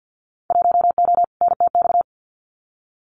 Morse code, heard as a keyed tone, about 50 wpm, 700 Hz. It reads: 1O NTX
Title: Winter Field Day Exchanges 50wpm